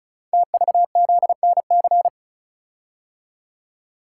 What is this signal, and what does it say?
Morse code, heard as a keyed tone, about 36 wpm, 700 Hz. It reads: TVZNC
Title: Code Groups - 5 Letter:Numbers - 35wpm